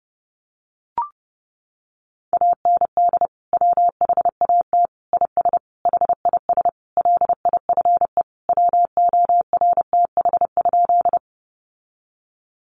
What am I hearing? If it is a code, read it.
Morse code, heard as a keyed tone, about 30 wpm, 700 Hz. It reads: AND WHAT IS HIS LIFE WORTH?
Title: Sentences from Top 800 Words 30wpm